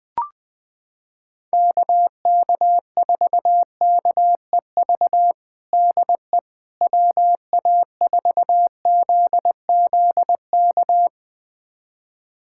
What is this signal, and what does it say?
Morse code, heard as a keyed tone, about 20 wpm, 700 Hz. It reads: KK4KEV DE WA4ZZK
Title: USA Callsign DE Callsign - 6 Characters - 20wpm